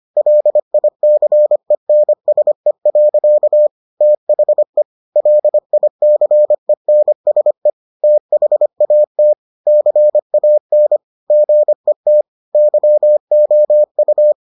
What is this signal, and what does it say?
Morse code, heard as a keyed tone, about 25 wpm, 600 Hz. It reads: LICENSE. THE LICENSE THAT CAN GET YOU